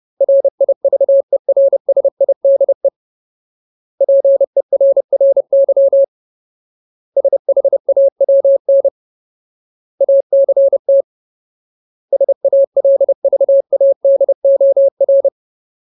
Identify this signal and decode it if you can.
Morse code, heard as a keyed tone, about 30 wpm, 550 Hz. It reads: RIVERSIDE PERRY SHAWN ACT SALVADOR
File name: Ultimate Word List / 30WPM Ultimate